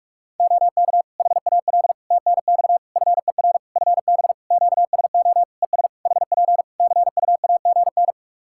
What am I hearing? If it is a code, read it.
Morse code, heard as a keyed tone, about 44 wpm, 700 Hz. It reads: OK HAL TNX FER FB QSO ES HP CUAGN
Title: Practice QSO - Exchange 4 - Ending - 45wpm